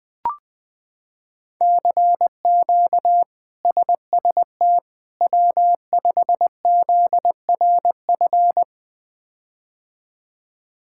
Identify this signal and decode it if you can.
Morse code, heard as a keyed tone, about 20 wpm, 700 Hz. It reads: CQ SST W5ZRF
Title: K1USN SST QSO Practice - 20wpm